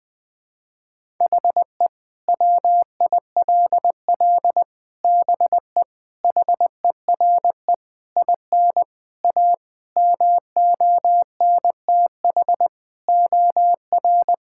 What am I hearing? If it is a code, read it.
Morse code, heard as a keyed tone, about 20 wpm, 700 Hz. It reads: HE WILL BE HERE IN A MONTH OR
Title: Sentences from Top 1000 Words - Review Part 5 - 20wpm